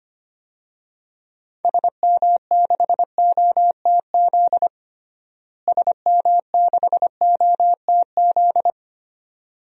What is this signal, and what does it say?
Morse code, heard as a keyed tone, about 25 wpm, 700 Hz. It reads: SM6OTZ SM6OTZ
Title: DXCC-100 Call Signs - 6 Characters - Repeated 2x - 25wpm